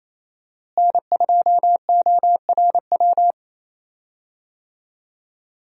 Morse code, heard as a keyed tone, about 28 wpm, 700 Hz. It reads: N2ORW
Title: USA Call Signs - 5 Characters - Repeated 3x - 28wpm